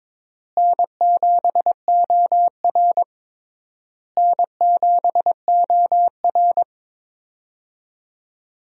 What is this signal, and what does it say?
Morse code, heard as a keyed tone, about 22 wpm, 700 Hz. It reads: N7OR N7OR
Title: USA Call Signs - 4 Characters - Repeated 3x - 22wpm